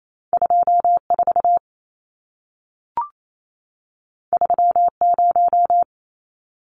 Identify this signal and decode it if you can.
Morse code, heard as a keyed tone, about 28 wpm, 700 Hz. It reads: 24 30
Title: Numbers - 2-Digit 28wpm